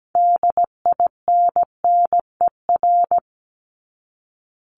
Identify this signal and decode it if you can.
Morse code, heard as a keyed tone, about 17 wpm, 700 Hz. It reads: DINNER
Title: Top 1751-2000 Words 17wpm